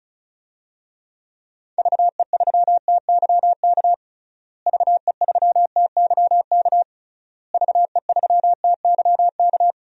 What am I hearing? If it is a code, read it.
Morse code, heard as a keyed tone, about 34 wpm, 700 Hz. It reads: VE3TYK VE3TYK VE3TYK
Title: DXCC-100 Call Signs - 6 Characters - Repeated 3x - 35wpm